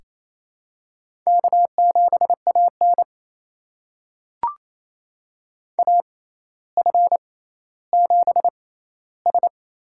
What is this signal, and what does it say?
Morse code, heard as a keyed tone, about 28 wpm, spaced as for 14 wpm, 700 Hz. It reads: K7AN AF7S
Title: USA Call Signs - 4 Characters - 6x Character Spacing - 28wpm